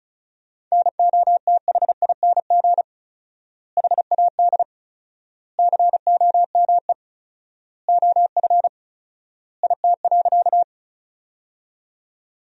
Morse code, heard as a keyed tone, about 34 wpm, 700 Hz. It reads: NOTHING HAD COME OF IT.
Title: 4x Word-Spacing - Sentences from Top 500 Words - 35wpm